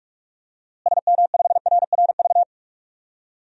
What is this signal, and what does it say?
Morse code, heard as a keyed tone, about 46 wpm, 700 Hz. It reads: SM5RRV
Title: DXCC-100 Call Signs - 6 Characters - Repeated 3x - 45wpm